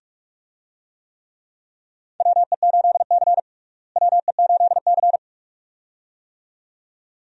Morse code, heard as a keyed tone, about 42 wpm, 700 Hz. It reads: WE8C WE8C
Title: USA Call Signs - 4 Characters - Repeated 2x - 45wpm